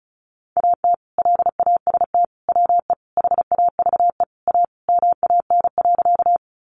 Morse code, heard as a keyed tone, about 34 wpm, 700 Hz. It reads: AT LAST WE HAVE A MAN.
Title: Sentences from Top 300 Words 35wpm